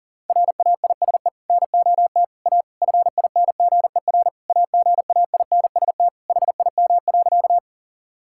Morse code, heard as a keyed tone, about 40 wpm, 700 Hz. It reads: RAISE NOT A FINGER AGAINST HIM.
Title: Sentences from Top 1600 Words 40wpm